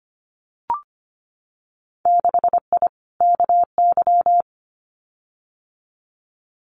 Morse code, heard as a keyed tone, about 25 wpm, 700 Hz. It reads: 6I KY
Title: Winter Field Day Exchanges Encore 25wpm